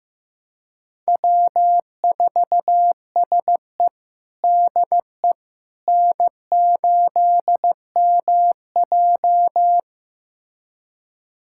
Morse code, heard as a keyed tone, about 15 wpm, 700 Hz. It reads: W4SE DE N8MJ
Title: USA Callsign DE Callsign - 4 Characters - 15wpm